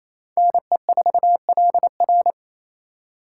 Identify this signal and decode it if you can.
Morse code, heard as a keyed tone, about 35 wpm, 700 Hz. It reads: NE4LR